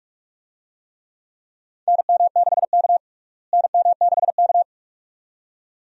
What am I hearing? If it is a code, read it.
Morse code, heard as a keyed tone, about 41 wpm, 700 Hz. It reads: NM6K NM6K
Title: USA Call Signs - 4 Characters - Repeated 2x - 45wpm